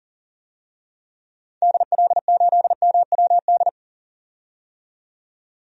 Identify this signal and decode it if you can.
Morse code, heard as a keyed tone, about 40 wpm, 700 Hz. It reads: DL8MWD
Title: DXCC-100 Call Signs - 6 Characters - 40wpm